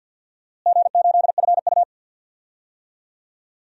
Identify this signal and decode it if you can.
Morse code, heard as a keyed tone, about 52 wpm, 700 Hz. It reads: G8FU
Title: DXCC-100 Call Signs - 4 Characters - Repeated 2x - 50wpm